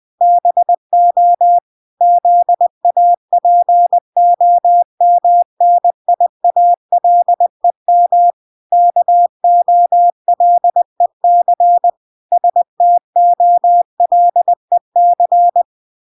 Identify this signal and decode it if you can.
Morse code, heard as a keyed tone, about 20 wpm, 700 Hz. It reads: BO ZAPOMNIALEM KOLEC STOLEC